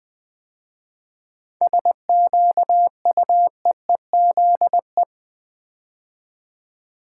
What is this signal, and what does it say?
Morse code, heard as a keyed tone, about 20 wpm, 690 Hz. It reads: SQUEEZE
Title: 7 Letter Words - First 3 Letters Given - 20wpm